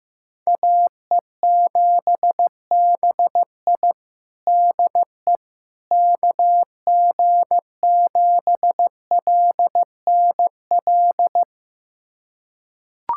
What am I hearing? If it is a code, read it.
Morse code, heard as a keyed tone, about 15 wpm, 700 Hz. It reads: AE7BI DE KG7LNL
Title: USA Callsign DE Callsign 15wpm